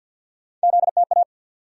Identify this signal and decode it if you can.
Morse code, heard as a keyed tone, about 48 wpm, 700 Hz. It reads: ZTA
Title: USA Call Sign Postfixes - 3 Letters 50wpm